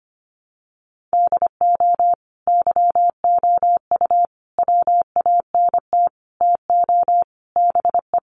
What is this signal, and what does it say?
Morse code, heard as a keyed tone, about 25 wpm, 700 Hz. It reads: DO YOU WANT TO BE
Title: Sentences from Top 1100 Words 25wpm